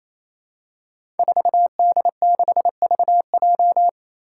Morse code, heard as a keyed tone, about 28 wpm, 700 Hz. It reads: 4D6VJ
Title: Five Challenging Letter-Number Combinations 28wpm